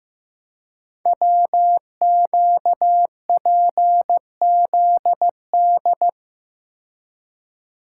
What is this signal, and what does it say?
Morse code, heard as a keyed tone, about 15 wpm, 700 Hz. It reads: WQPZD
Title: Code Groups - 5 Letter:Numbers - 15wpm